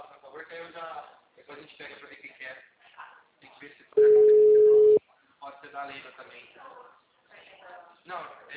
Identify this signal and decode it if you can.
Ringback tone (Europe, 425 Hz), with speech.
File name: out- -2015- - - .35387